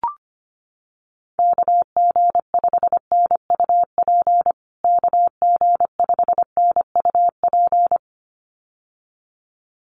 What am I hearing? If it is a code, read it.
Morse code, heard as a keyed tone, about 25 wpm, 700 Hz. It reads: KG5NUP KG5NUP
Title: USA Call Signs Repeated 2x 25wpm